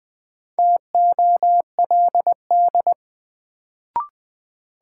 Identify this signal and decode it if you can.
Morse code, heard as a keyed tone, about 20 wpm, 700 Hz. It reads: TOLD
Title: Sentences from Top 400 Words - New Word Drill - 20wpm